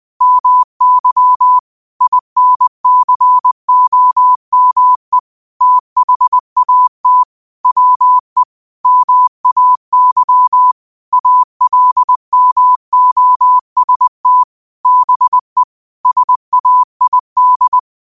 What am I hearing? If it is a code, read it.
Morse code, heard as a keyed tone, about 20 wpm, 1000 Hz. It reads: MY INCOME THAT WE MAY ALMOST BE SAID
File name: qotdfiles / 20wpm